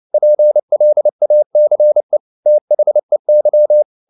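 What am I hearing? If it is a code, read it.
Morse code, heard as a keyed tone, about 29 wpm, 600 Hz. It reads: PLACE THEY
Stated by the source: Morse code at 29 wpm, sending Ken Follett's novel Never in English, from CW med Gnister.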